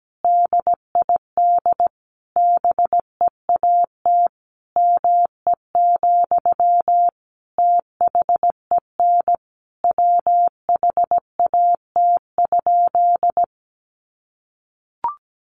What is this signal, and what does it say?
Morse code, heard as a keyed tone, about 17 wpm, 700 Hz. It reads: DID BEAT ME, THEN WHAT?
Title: Sentences from Top 1300 Words 17wpm